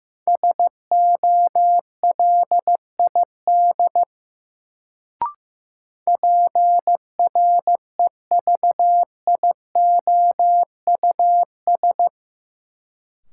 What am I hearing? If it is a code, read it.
Morse code, heard as a keyed tone, about 15 wpm, 700 Hz. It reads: SOLID PREVIOUS
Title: Sentences from Top 1500 Words - New Word Drill - 15wpm